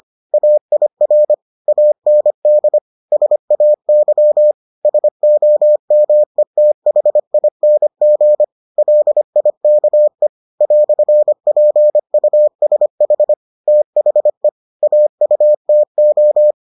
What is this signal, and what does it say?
Morse code, heard as a keyed tone, about 25 wpm, 600 Hz. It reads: AIR AND SAY SOMETHING LIKE "PUSH THE AUTO